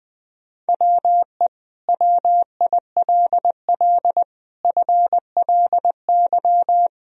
Morse code, heard as a keyed tone, about 20 wpm, 700 Hz. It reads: WE WILL FLY
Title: Sentences from Top 1000 Words - Review Part 2 - 20wpm